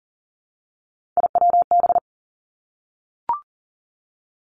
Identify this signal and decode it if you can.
Morse code, heard as a keyed tone, about 41 wpm, 700 Hz. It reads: IWB
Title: Three Challenging Letter-Number Combinations 40wpm